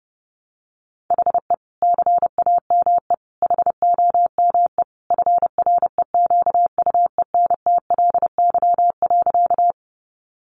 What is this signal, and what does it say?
Morse code, heard as a keyed tone, about 30 wpm, 700 Hz. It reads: HE CAME HOME FREQUENTLY.
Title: Sentences from Top 1300 Words 30wpm